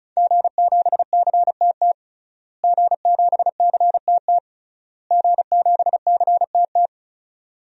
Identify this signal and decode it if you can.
Morse code, heard as a keyed tone, about 34 wpm, 700 Hz. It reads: G7CTT G7CTT G7CTT
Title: DXCC-100 Call Signs - 5 Characters - Repeated 3x - 35wpm